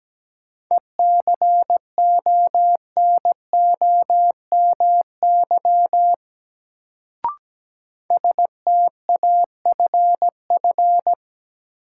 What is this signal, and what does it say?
Morse code, heard as a keyed tone, about 17 wpm, 700 Hz. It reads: ECONOMY STAFF
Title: Sentences from Top 1200 Words - New Word Drill Alt - 17wpm